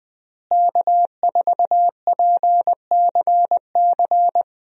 Morse code, heard as a keyed tone, about 20 wpm, 700 Hz. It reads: K4PCC
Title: USA Call Signs - 5 Characters - Repeated 3x - 20wpm